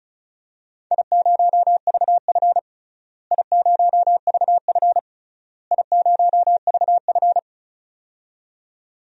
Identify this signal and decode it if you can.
Morse code, heard as a keyed tone, about 34 wpm, 700 Hz. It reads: I0VF I0VF I0VF
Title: DXCC-100 Call Signs - 4 Characters - Repeated 3x - 35wpm